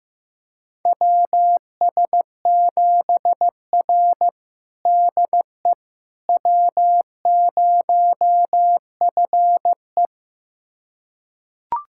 Morse code, heard as a keyed tone, about 15 wpm, 700 Hz. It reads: WS7R DE W0FE
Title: USA Callsign DE Callsign - 4 Characters - 15wpm